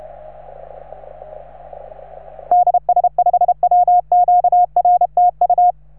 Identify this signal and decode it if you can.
Morse code, heard as a keyed tone, about 33 wpm, 710 Hz. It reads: DS5WQR TU
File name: RS3A-20